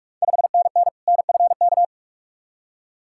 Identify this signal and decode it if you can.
Morse code, heard as a keyed tone, about 46 wpm, 690 Hz. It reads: 5NN NFK
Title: Ontario QSO Party Exchange 45wpm